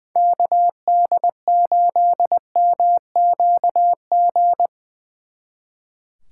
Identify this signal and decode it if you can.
Morse code, heard as a keyed tone, about 22 wpm, 700 Hz. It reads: KD8MQG
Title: Speed Racing - USA 2x3 Call Signs 20wpm